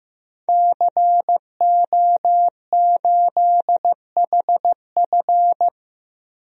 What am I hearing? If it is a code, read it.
Morse code, heard as a keyed tone, about 15 wpm, 700 Hz. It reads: CO8HF
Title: DXCC-100 Call Signs - 5 Characters - 15wpm